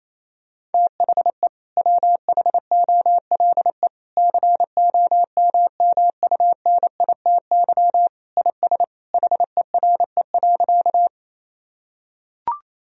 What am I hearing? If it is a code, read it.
Morse code, heard as a keyed tone, about 28 wpm, 700 Hz. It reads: THE WHOLE COMMUNITY IS HERE.
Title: Sentences from Top 1000 Words - Review Part 4 - 28wpm